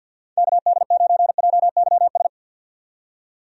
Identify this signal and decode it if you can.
Morse code, heard as a keyed tone, about 47 wpm, 700 Hz. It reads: KD9JYS
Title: USA Call Signs - 6 Characters - 50wpm